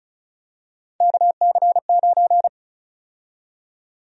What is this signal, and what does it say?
Morse code, heard as a keyed tone, about 36 wpm, 700 Hz. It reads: KC9
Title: USA Call Sign Prefixes - Repeated 2x - 35wpm